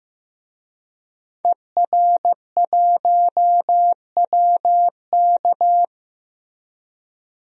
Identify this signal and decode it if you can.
Morse code, heard as a keyed tone, about 15 wpm, 700 Hz. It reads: ER1WK
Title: DXCC-100 Call Signs - 5 Characters - Repeated 3x - 15wpm